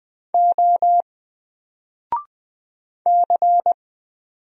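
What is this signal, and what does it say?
Morse code, heard as a keyed tone, about 20 wpm, 700 Hz. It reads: O C
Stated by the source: Single letter and number Morse drill at 20 wpm, from N4LCD.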